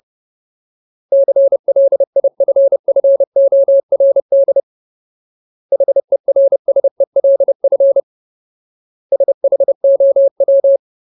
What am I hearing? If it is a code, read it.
Morse code, heard as a keyed tone, about 30 wpm, 550 Hz. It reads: CLIFFORD HERSELF SHOW
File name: Ultimate Word List / 30WPM Ultimate